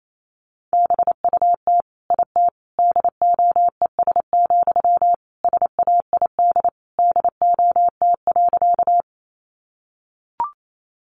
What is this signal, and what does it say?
Morse code, heard as a keyed tone, about 28 wpm, 700 Hz. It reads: BUT IT DOES, SAID DOT.
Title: Sentences from Top 1500 Words 28wpm